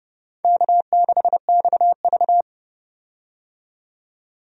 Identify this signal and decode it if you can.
Morse code, heard as a keyed tone, about 29 wpm, 700 Hz. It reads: K6XV
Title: USA Call Signs - 4 Characters - 30wpm